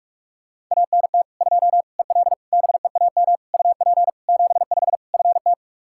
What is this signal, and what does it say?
Morse code, heard as a keyed tone, about 41 wpm, 700 Hz. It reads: ANT 2 EL BEAM UP 75 FT
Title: K1USN SST QSO Practice 45wpm